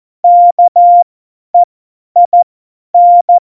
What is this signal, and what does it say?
Morse code, heard as a keyed tone, about 14 wpm, spaced as for 10 wpm, 700 Hz. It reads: KEIN
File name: Morsekurs / Witze / Witz 05. Februar 2019, 50 F70